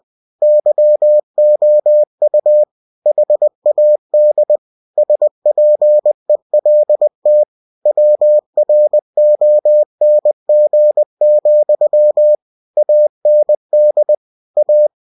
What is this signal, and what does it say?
Morse code, heard as a keyed tone, about 20 wpm, 600 Hz. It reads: YOU HAD SPELT WRONG, AND A
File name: cw-books / winnie-the-pooh / winnie-20wpm-page-0109